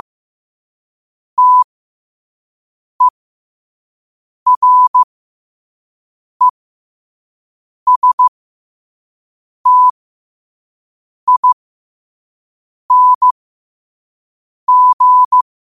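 Morse code, heard as a keyed tone, about 15 wpm, spaced as for 5 wpm, 1000 Hz. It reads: TERESTING
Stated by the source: Morse code practice at 5 wpm.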